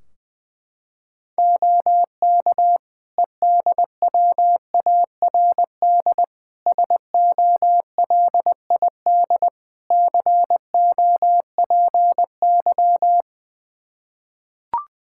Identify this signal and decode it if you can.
Morse code, heard as a keyed tone, about 20 wpm, 700 Hz. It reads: OK EDWARD SOLID COPY
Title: Practice QSO - Exchange 3 - More Info - 20wpm